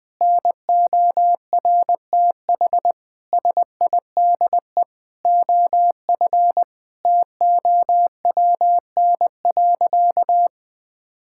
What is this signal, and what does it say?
Morse code, heard as a keyed tone, about 19 wpm, 700 Hz. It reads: NORTH SIDE OF TOWN.
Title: Sentences with 6-Word Phrases - 20wpm